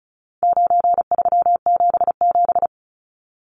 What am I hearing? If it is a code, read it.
Morse code, heard as a keyed tone, about 34 wpm, 700 Hz. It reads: 9377
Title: Numbers - 4-Digit 35wpm